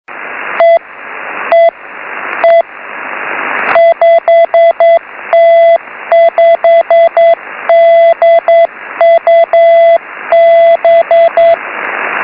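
Morse code, heard as a keyed tone, about 9 wpm, 670 Hz, with radio noise. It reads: E E E 5T5DUB
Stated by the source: Beacon 5T5DUB heard on 6m by IW0FFK.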